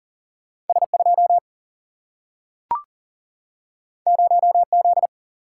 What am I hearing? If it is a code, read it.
Morse code, heard as a keyed tone, about 40 wpm, 700 Hz. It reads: S 2 0 Z
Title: Single-Letter Single-Number 40wpm